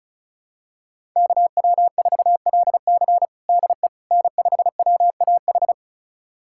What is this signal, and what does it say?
Morse code, heard as a keyed tone, about 36 wpm, 700 Hz. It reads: KW4LC DE N5WAH
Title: USA Callsign DE Callsign - 5 Characters - 35wpm